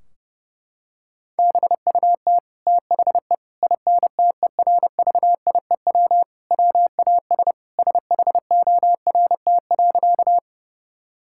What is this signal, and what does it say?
Morse code, heard as a keyed tone, about 32 wpm, 700 Hz. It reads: BUT THE INTERVIEW WAS SHORT.
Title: Sentences from Top 1600 Words 30wpm